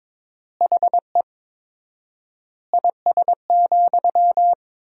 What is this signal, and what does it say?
Morse code, heard as a keyed tone, about 22 wpm, 700 Hz. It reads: HE IS,
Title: 4x Word-Spacing - Sentences from Top 100 Words - 22wpm